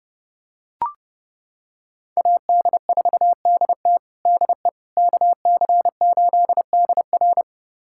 Morse code, heard as a keyed tone, about 30 wpm, 700 Hz. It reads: AD4DT DE KC8DR
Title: USA Callsign DE Callsign - 5 Characters - 30wpm